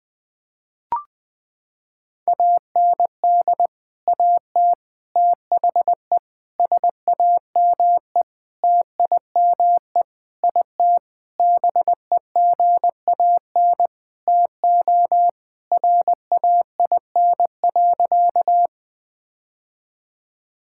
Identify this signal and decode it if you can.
Morse code, heard as a keyed tone, about 20 wpm, 700 Hz. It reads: AND AT THE SAME TIME IT BEGAN TO RAIN.
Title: Sentences with 5-Word Phrases - 20wpm